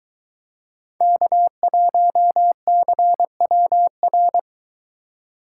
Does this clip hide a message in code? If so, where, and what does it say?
Yes: Morse code from 1.0 to 4.4 s, heard as a keyed tone, about 21 wpm, 700 Hz. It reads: K1CWR